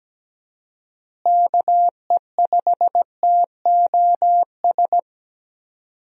Morse code, heard as a keyed tone, about 17 wpm, 700 Hz. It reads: KE5TOS
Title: USA Call Signs - 6 Characters - Repeated 2x - 17wpm